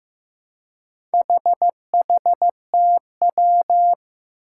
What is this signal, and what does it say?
Morse code, heard as a keyed tone, about 15 wpm, 700 Hz. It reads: HHTW
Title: Four Challenging Letter-Number Combinations 15wpm